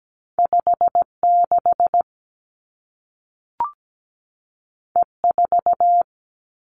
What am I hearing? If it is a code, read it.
Morse code, heard as a keyed tone, about 17 wpm, 700 Hz. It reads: 5 6 E 4
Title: Two Challenging Letter-Number Combinations 17wpm